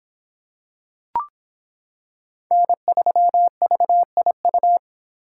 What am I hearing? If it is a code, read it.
Morse code, heard as a keyed tone, about 25 wpm, 700 Hz. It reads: N3VIU